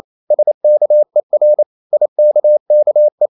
Morse code, heard as a keyed tone, about 28 wpm, 600 Hz. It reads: SKER IKKE
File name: Ferie QRQ 28wpm